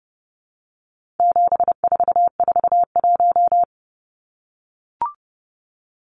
Morse code, heard as a keyed tone, about 29 wpm, 700 Hz. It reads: 7441
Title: Numbers - 4-Digit 30wpm